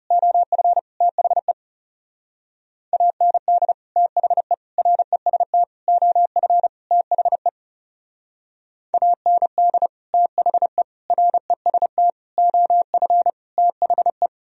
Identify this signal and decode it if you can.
Morse code, heard as a keyed tone, about 39 wpm, 700 Hz. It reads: OF THE AND THE REST OF THE AND THE REST OF THE